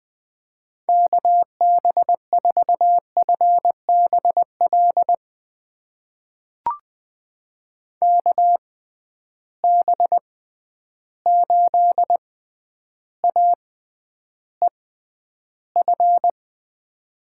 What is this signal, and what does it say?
Morse code, heard as a keyed tone, about 20 wpm, spaced as for 10 wpm, 700 Hz. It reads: KB4FBL KB8AEF
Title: USA Call Signs - 6 Characters - 6x Character Spacing - 20wpm